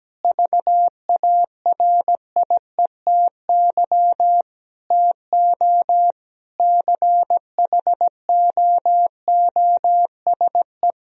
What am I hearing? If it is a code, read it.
Morse code, heard as a keyed tone, about 17 wpm, 700 Hz. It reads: VARIETY TO CHOOSE
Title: Sentences from Top 1000 Words - Review Part 4 - 17wpm